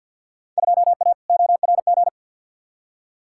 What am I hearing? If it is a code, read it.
Morse code, heard as a keyed tone, about 47 wpm, 700 Hz. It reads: 2A ORG
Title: Field Day Exchanges Encore - 50wpm